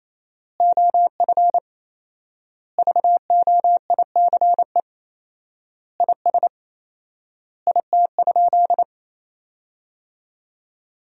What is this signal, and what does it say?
Morse code, heard as a keyed tone, about 28 wpm, 700 Hz. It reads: OF VOICE IS IT?
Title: 4x Word-Spacing - Sentences from Top 500 Words - 28wpm